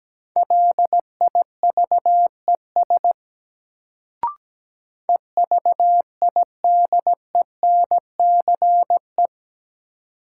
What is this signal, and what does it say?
Morse code, heard as a keyed tone, about 17 wpm, 700 Hz. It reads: LIVES EVIDENCE
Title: Sets of 5 Words Encore - New Word Drill - 17wpm